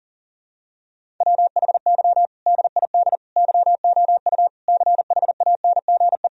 Morse code, heard as a keyed tone, about 40 wpm, 700 Hz. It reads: WHY DID YOU CHANGE
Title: Sentences from Top 1000 Words - Review Part 1 - 40wpm